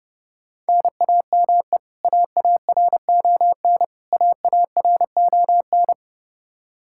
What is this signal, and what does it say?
Morse code, heard as a keyed tone, about 30 wpm, 700 Hz. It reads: NAME AARON AARON
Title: Practice QSO - Exchange 1 - Beginning - 30wpm